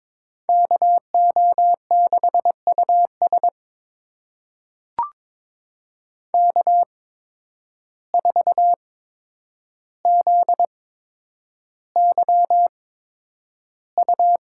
Morse code, heard as a keyed tone, about 22 wpm, spaced as for 9 wpm, 700 Hz. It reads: KO6US K4ZYU
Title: USA Call Signs - 5 Characters - 8x Character Spacing - 22wpm